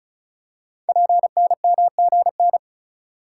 Morse code, heard as a keyed tone, about 36 wpm, 700 Hz. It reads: PNMGN
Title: Code Groups - 5 Letters - 35wpm